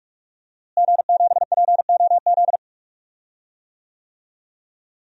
Morse code, heard as a keyed tone, about 46 wpm, 700 Hz. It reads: G7POZ
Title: DXCC-100 Call Signs - 5 Characters - Repeated 3x - 45wpm